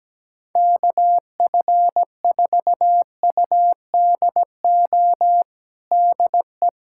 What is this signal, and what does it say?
Morse code, heard as a keyed tone, about 17 wpm, 700 Hz. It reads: KF4UDO DE
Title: USA Callsign DE Callsign 17wpm